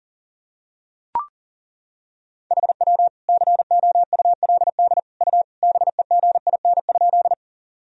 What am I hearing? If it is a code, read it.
Morse code, heard as a keyed tone, about 37 wpm, 700 Hz. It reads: HW COULD U BEGIN?
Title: Sentences from Top 300 Words and Abbreviations 40wpm